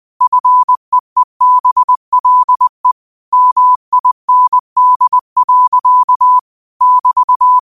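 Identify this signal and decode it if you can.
Morse code, heard as a keyed tone, about 20 wpm, 1000 Hz. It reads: FEEBLE MIND. <BT>
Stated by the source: Quote of the day sent in Morse at 20 words per minute.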